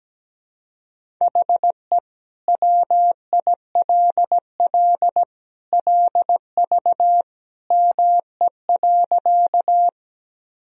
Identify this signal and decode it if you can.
Morse code, heard as a keyed tone, about 17 wpm, 700 Hz. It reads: HE WILL LV ME.
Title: Sentences from Top 400 Words and Abbreviations 17wpm